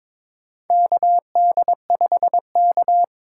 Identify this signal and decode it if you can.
Morse code, heard as a keyed tone, about 22 wpm, 700 Hz. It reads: KD5K
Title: USA Call Signs - 2x Character Spacing - 22wpm